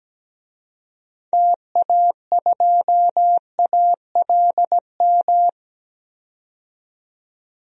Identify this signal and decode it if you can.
Morse code, heard as a keyed tone, about 17 wpm, 700 Hz. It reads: TA2ALM
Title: DXCC-100 Call Signs - 6 Characters - 17wpm